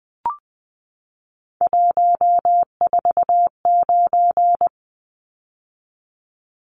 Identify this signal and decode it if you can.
Morse code, heard as a keyed tone, about 20 wpm, 700 Hz. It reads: 149
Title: Numbers - 3-Digit 20wpm